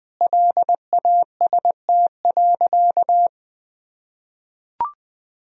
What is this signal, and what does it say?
Morse code, heard as a keyed tone, about 20 wpm, 700 Hz. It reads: LAST.
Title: Sentences from Top 300 Words and Abbreviations 20wpm